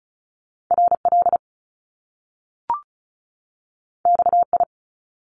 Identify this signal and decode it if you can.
Morse code, heard as a keyed tone, about 36 wpm, 700 Hz. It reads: RL XI
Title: Top Two Letter Combinations in Words 35wpm